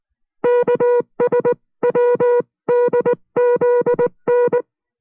Morse code, heard as a keyed tone, about 19 wpm, 490 Hz. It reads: KSWDZN